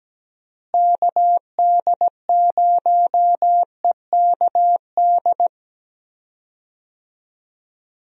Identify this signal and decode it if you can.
Morse code, heard as a keyed tone, about 17 wpm, 700 Hz. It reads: KD0EKD
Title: USA Call Signs - 6 Characters - Repeated 3x - 17wpm